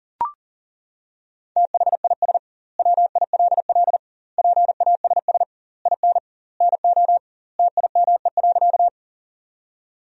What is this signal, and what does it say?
Morse code, heard as a keyed tone, about 40 wpm, 700 Hz. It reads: THIS WILL PASS IN NO TIME.
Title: 2x Word-Spacing - Sentences from Top 400 Words - 40wpm